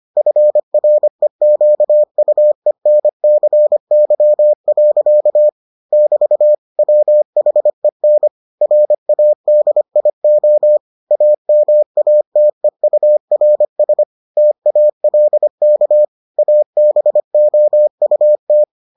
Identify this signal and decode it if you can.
Morse code, heard as a keyed tone, about 25 wpm, 600 Hz. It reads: FREQUENCY. <BT> WHEN RADIO AMATEURS TALK ABOUT